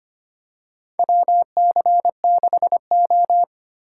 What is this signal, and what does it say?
Morse code, heard as a keyed tone, about 25 wpm, 700 Hz. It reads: WC6O
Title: USA Call Signs - 3x Character Spacing - 25wpm